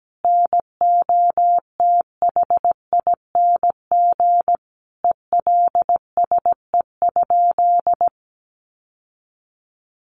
Morse code, heard as a keyed tone, about 17 wpm, 700 Hz. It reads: NOTHING ELSE?
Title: Sentences from Top 800 Words 17wpm